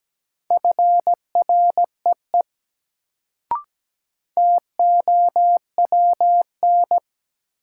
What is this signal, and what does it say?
Morse code, heard as a keyed tone, about 17 wpm, 700 Hz. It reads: FREE TOWN
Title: Sentences from Top 500 Words - New Word Drill - 17wpm